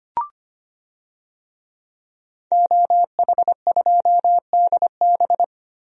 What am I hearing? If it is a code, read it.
Morse code, heard as a keyed tone, about 25 wpm, 700 Hz. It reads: OH2DB
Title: DXCC-100 Call Signs - 5 Characters - Repeated 2x - 25wpm